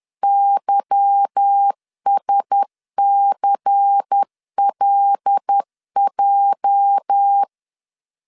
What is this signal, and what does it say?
Morse code, heard as a keyed tone, about 10 wpm, 800 Hz. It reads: YSCLJ
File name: LEZIONI / 8 YJ45 / 8 YJ 10wpm 2651